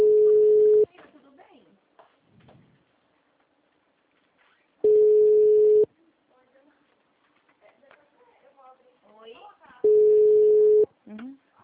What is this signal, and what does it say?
Ringback tone (Europe, 425 Hz), with speech.
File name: out- -2009- - - .31924